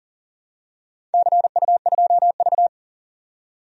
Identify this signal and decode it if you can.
Morse code, heard as a keyed tone, about 40 wpm, 700 Hz. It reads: CU2V
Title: DXCC-100 Call Signs - 4 Characters - 40wpm